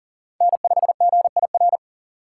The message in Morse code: N5GIR

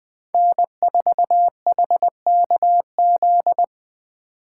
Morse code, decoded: N4HKZ

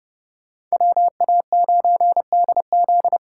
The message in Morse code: WA9DZ

